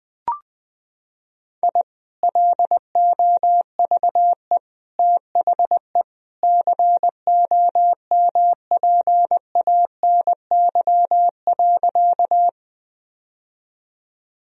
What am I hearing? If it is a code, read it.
Morse code, heard as a keyed tone, about 21 wpm, 700 Hz. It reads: I LOVE THE COMPANY.
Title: Sentences from Top 1000 Words - Review Part 3 - 20wpm